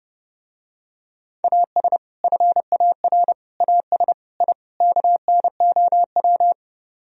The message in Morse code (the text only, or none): AS FAR AS I KNOW